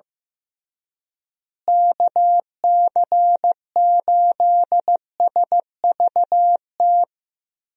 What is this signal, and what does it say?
Morse code, heard as a keyed tone, about 15 wpm, 700 Hz. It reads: KC8SVT
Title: USA Call Signs - 6 Characters - Repeated 3x - 15wpm